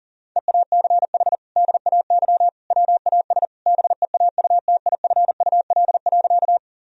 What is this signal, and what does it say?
Morse code, heard as a keyed tone, about 40 wpm, 700 Hz. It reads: EACH DAY WAS BEAUTIFUL.
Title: Sentences from Top 1000 Words - Review Part 1 - 40wpm